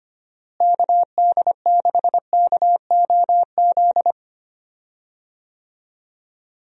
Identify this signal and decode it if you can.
Morse code, heard as a keyed tone, about 25 wpm, 700 Hz. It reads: KD6KOZ
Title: USA Call Signs - 6 Characters - Repeated 2x - 25wpm